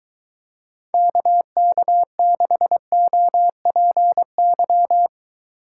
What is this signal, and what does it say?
Morse code, heard as a keyed tone, about 23 wpm, 700 Hz. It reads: KK6OPY